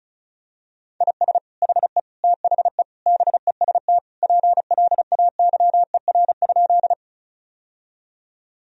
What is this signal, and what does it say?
Morse code, heard as a keyed tone, about 36 wpm, 700 Hz. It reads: IS HE THE BEST PLAYER?
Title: Sentences from Top 1000 Words - Review Part 4 - 35wpm